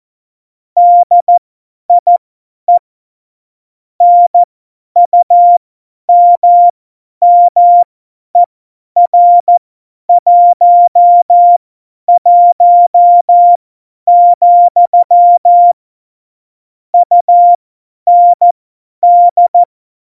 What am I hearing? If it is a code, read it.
Morse code, heard as a keyed tone, about 14 wpm, spaced as for 10 wpm, 700 Hz. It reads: DIE NUMMER 11, UND
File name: Morsekurs / Witze / Witz 11 August 2020, 50 F70